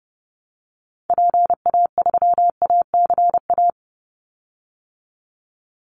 Morse code, heard as a keyed tone, about 30 wpm, 700 Hz. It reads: PA3ACA
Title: Call Signs Repeated 3x - 30wpm